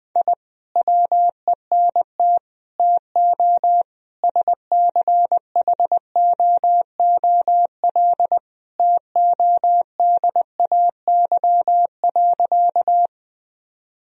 Morse code, heard as a keyed tone, about 20 wpm, 700 Hz. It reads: I WENT TO SCHOOL TODAY.
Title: Sentences from Top 1000 Words - Review Part 1 - 20wpm